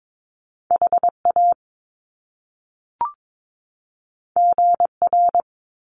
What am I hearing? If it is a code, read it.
Morse code, heard as a keyed tone, about 22 wpm, 700 Hz. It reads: HA GR
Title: Top Two Letter Combinations in Words 22wpm